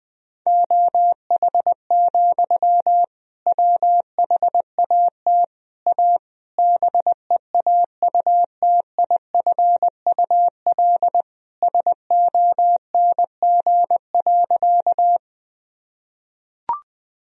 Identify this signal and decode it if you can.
Morse code, heard as a keyed tone, about 20 wpm, 700 Hz. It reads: OH, WHAT A BEAUTIFUL SONG.
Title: Sentences from Top 1000 Words - Review Part 2 - 20wpm